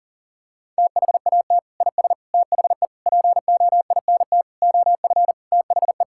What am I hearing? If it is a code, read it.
Morse code, heard as a keyed tone, about 37 wpm, 700 Hz. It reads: THAT IS THE POINT OF THE
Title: Sentences from Top 1000 Words - Review Part 5 - 40wpm